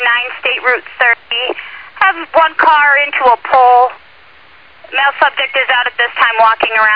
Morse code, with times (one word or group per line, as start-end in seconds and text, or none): none